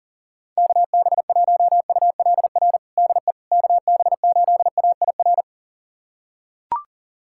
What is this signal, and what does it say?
Morse code, heard as a keyed tone, about 40 wpm, 700 Hz. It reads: KB1ULR DE KB8AIR
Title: USA Callsign DE Callsign 40wpm